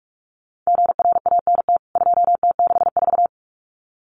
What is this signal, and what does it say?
Morse code, heard as a keyed tone, about 44 wpm, 700 Hz. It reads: GRANT 2T64
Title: CWT Exchange 45wpm